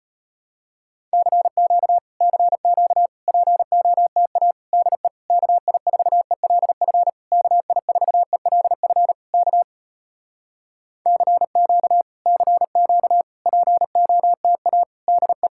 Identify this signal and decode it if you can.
Morse code, heard as a keyed tone, about 34 wpm, 700 Hz. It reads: CQ CQ POTA DE KI4ELF KI4ELF K CQ CQ POTA DE